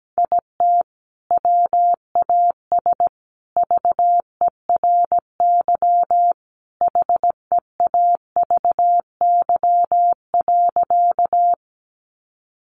Morse code, heard as a keyed tone, about 17 wpm, 700 Hz. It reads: IT WAS VERY HEAVY.
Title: Sentences from Top 1100 Words 17wpm